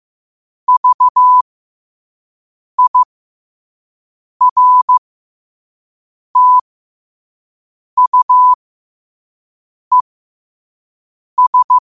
Morse code, heard as a keyed tone, about 15 wpm, spaced as for 5 wpm, 1000 Hz. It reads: VIRTUES
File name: qotdfiles / 5wpm